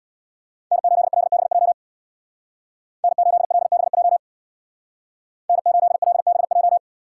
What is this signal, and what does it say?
Morse code, heard as a keyed tone, about 59 wpm, 700 Hz. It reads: N8LBJ N8LBJ N8LBJ
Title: Speed Racing - USA 1x3 Call Signs 50wpm